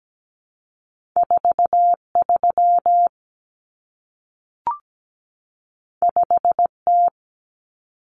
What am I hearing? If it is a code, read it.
Morse code, heard as a keyed tone, about 17 wpm, 700 Hz. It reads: 43 5T
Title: Common Cut Numbers - 2-Digit 17wpm